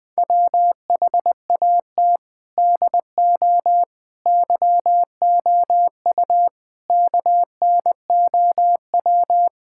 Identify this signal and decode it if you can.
Morse code, heard as a keyed tone, about 20 wpm, 700 Hz. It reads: WHAT DO YOU KNOW
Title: Sentences from Top 1000 Words - Review Part 6 - 20wpm